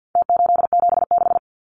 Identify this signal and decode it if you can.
Morse code, heard as a keyed tone, about 50 wpm, 700 Hz. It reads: T876
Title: Numbers - Contest Style - 4-Digit 50wpm